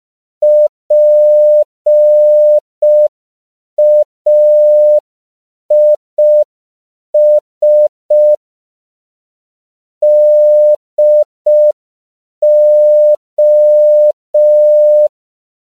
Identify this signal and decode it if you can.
Morse code, heard as a keyed tone, about 5 wpm, 600 Hz. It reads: PAIS DO
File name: CW / 5PPM